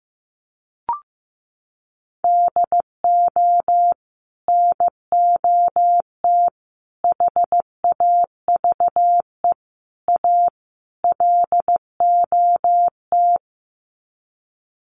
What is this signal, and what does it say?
Morse code, heard as a keyed tone, about 15 wpm, 700 Hz. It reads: DO NOT HAVE A LOT
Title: Sets of 5 Words Encore 15wpm